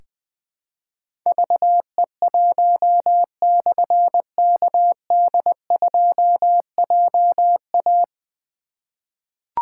Morse code, heard as a keyed tone, about 20 wpm, 700 Hz. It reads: VE1/KD2JA
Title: Difficult Call Signs 20wpm